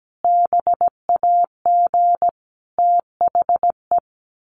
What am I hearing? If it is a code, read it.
Morse code, heard as a keyed tone, about 17 wpm, 700 Hz. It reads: BAG THE
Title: A Sound Of Thunder 17wpm